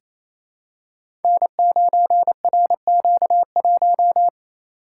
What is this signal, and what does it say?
Morse code, heard as a keyed tone, about 27 wpm, 700 Hz. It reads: N9RQ1
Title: Code Groups - 5 Letter:Numbers - 28wpm